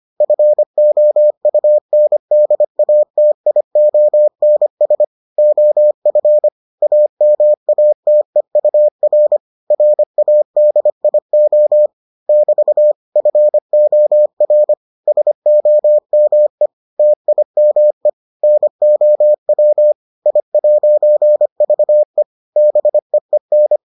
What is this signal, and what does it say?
Morse code, heard as a keyed tone, about 25 wpm, 600 Hz. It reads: FOUNDATIONS OF AMATEUR RADIO <BT> FOR SOME TIME NOW I'VE BEEN